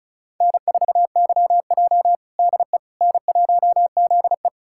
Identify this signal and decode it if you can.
Morse code, heard as a keyed tone, about 34 wpm, 700 Hz. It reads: N4YJ DE N1ZE
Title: USA Callsign DE Callsign - 4 Characters - 35wpm